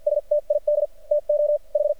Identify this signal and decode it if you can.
Morse code, heard as a keyed tone, about 50 wpm, spaced as for 41 wpm, 590 Hz. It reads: STID TO H
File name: archived / CRSnet